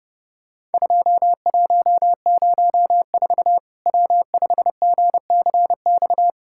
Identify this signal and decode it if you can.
Morse code, heard as a keyed tone, about 30 wpm, 700 Hz. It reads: 2104 W5GCX
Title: Worked All Europe DX Contest - QTC Exchange - 30wpm